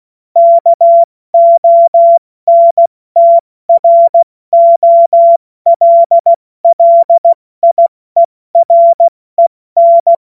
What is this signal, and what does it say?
Morse code, heard as a keyed tone, about 16 wpm, spaced as for 14 wpm, 680 Hz. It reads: KONTROLLIEREN
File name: Morsekurs / Witze / Witz 05 Dezember 2023 70 F80